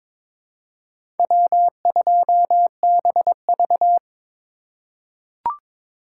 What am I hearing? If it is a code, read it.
Morse code, heard as a keyed tone, about 22 wpm, 700 Hz. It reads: W2BV
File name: practice / 22wpm / Four Letter-Number Combinations - 22wpm